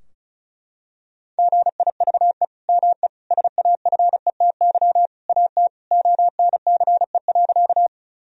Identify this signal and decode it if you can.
Morse code, heard as a keyed tone, about 36 wpm, 700 Hz. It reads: GIVE ME SAFETY AT ONCE.
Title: Sentences from Top 1000 Words - Review Part 4 - 35wpm